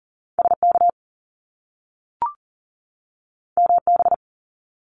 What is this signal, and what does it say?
Morse code, heard as a keyed tone, about 41 wpm, 700 Hz. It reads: SK MB
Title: Top Two Letter Combinations in Words 40wpm